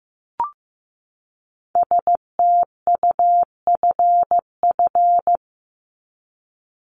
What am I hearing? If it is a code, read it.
Morse code, heard as a keyed tone, about 15 wpm, 700 Hz. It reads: STUFF
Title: Top 1251-1500 Words 15wpm